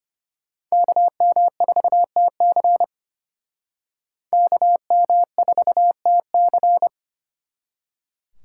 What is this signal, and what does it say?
Morse code, heard as a keyed tone, about 33 wpm, 700 Hz. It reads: KM4TC KM4TC